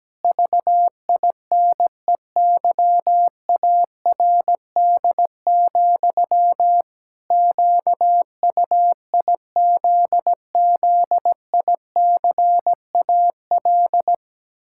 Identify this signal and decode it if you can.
Morse code, heard as a keyed tone, about 17 wpm, 700 Hz. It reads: VINEYARD, QUIZZICAL
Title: Pangram Challenge v2 - 17wpm